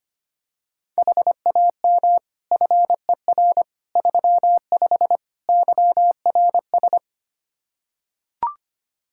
Morse code, heard as a keyed tone, about 25 wpm, 700 Hz. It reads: HAM FER 35 YRS
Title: K1USN SST QSO Practice 25wpm